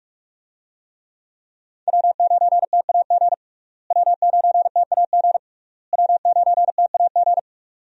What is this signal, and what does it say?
Morse code, heard as a keyed tone, about 41 wpm, 700 Hz. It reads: W9TAG W9TAG W9TAG
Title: USA Call Signs - 5 Characters - Repeated 3x - 45wpm